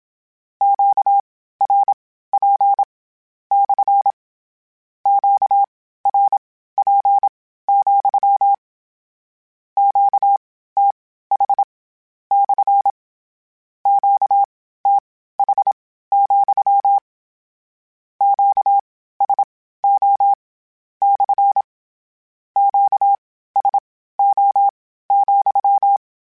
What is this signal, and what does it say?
Morse code, heard as a keyed tone, about 27 wpm, spaced as for 15 wpm, 800 Hz. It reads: QRP / QRP, QTH / QTH, QSO / QSO,
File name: Q Calls / Q Calls-mp3s / Q CALLS SPSL @25